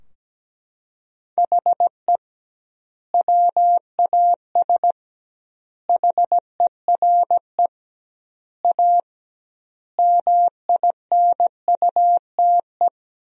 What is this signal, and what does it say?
Morse code, heard as a keyed tone, about 17 wpm, 700 Hz. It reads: HE WAS HERE A MINUTE
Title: 2x Word-Spacing - Sentences from Top 500 Words - 17wpm